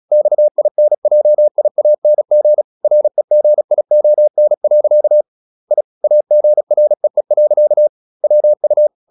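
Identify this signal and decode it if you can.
Morse code, heard as a keyed tone, about 36 wpm, 600 Hz. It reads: XINJIANG REGION. I AGREE. WU
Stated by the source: Morse practice at 36 wpm, text of Ken Follett's novel Never, from CW med Gnister.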